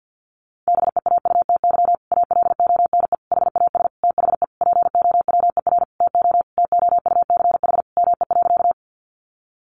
Morse code, heard as a keyed tone, about 50 wpm, 700 Hz. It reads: BEAUTY ALONE HAS THE POWER TO TOUCH ME.
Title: Sentences from Top 1200 Words 50wpm